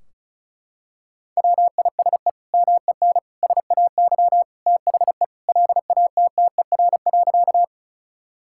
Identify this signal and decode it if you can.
Morse code, heard as a keyed tone, about 36 wpm, 700 Hz. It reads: WISE MEN SAY THE LATTER.
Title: Sentences from Top 1600 Words 35wpm